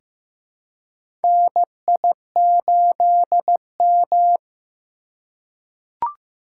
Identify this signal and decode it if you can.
Morse code, heard as a keyed tone, about 15 wpm, 700 Hz. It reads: NI8M
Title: USA Call Signs - 4 Characters - 3x Character Spacing - 15wpm